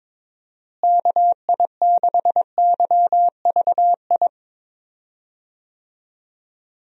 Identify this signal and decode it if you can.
Morse code, heard as a keyed tone, about 22 wpm, 700 Hz. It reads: KI6YVI
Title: USA Call Signs - 6 Characters - Repeated 2x - 22wpm